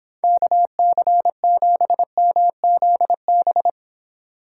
Morse code, heard as a keyed tone, about 26 wpm, 700 Hz. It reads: KC7MZB